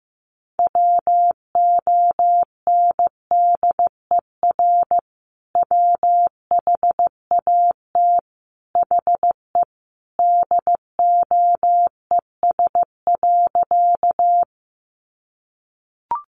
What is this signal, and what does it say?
Morse code, heard as a keyed tone, about 15 wpm, 700 Hz. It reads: WONDER WHAT HE DOES.
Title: Sentences from Top 500 Words 15wpm